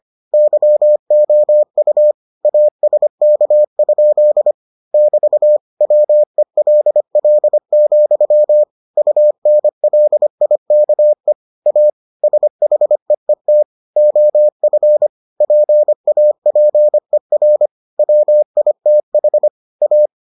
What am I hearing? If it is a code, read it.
Morse code, heard as a keyed tone, about 25 wpm, 600 Hz. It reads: YOU ASK? <BT> WELL, UNLIKE A SHEET OF PAPER WITH A